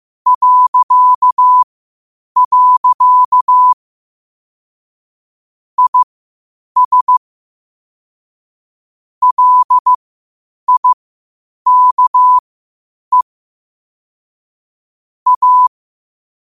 Morse code, heard as a keyed tone, about 15 wpm, spaced as for 8 wpm, 1000 Hz. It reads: .. IS LIKE A
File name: qotdfiles / 8wpm